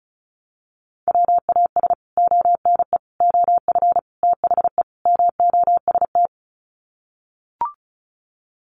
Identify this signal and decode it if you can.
Morse code, heard as a keyed tone, about 34 wpm, 700 Hz. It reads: WAS ONE OF THE MOST
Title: Sets of 5 Words 35wpm